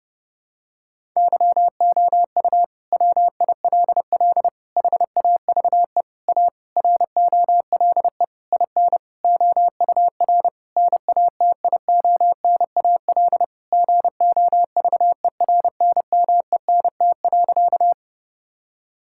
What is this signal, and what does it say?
Morse code, heard as a keyed tone, about 32 wpm, 700 Hz. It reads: YOU WILL HAVE A ROLE IN OUR NATIONAL GOVERNMENT.
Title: Sentences from Top 1000 Words - Review Part 6 - 30wpm